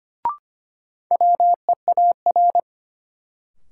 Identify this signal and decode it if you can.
Morse code, heard as a keyed tone, about 25 wpm, 700 Hz. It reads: WEAR
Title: Sentences from Top 1500 Words - New Word Drill Alt - 25wpm